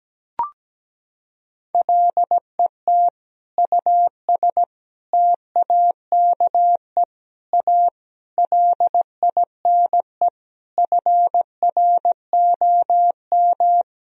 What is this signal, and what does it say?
Morse code, heard as a keyed tone, about 17 wpm, 700 Hz. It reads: LET US TAKE A LINE FROM
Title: Sentences from Top 1000 Words - Review Part 1 - 17wpm